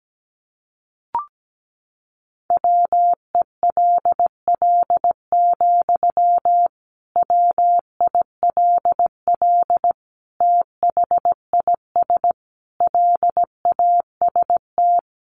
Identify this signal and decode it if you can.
Morse code, heard as a keyed tone, about 17 wpm, 700 Hz. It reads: WELL, WILL THIS LAST
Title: Sentences from Top 1300 Words 17wpm